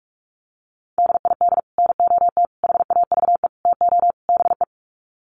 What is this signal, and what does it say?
Morse code, heard as a keyed tone, about 44 wpm, 700 Hz. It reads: DID NOT HAVE TO BE
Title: Sets of 5 Words Encore2 45wpm